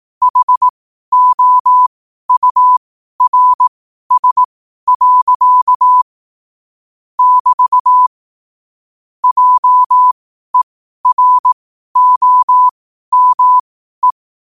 Morse code, heard as a keyed tone, about 18 wpm, spaced as for 12 wpm, 1000 Hz. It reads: HOURS. <BT> JEROME
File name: qotdfiles / 12wpm